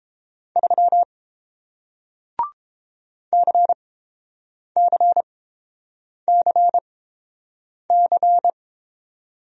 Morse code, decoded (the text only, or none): 3 CCCC